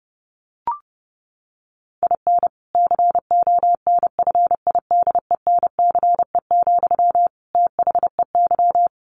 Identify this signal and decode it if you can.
Morse code, heard as a keyed tone, about 30 wpm, 700 Hz. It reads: IN CONFIDENCE, THEY
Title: Sentences from Top 1500 Words 30wpm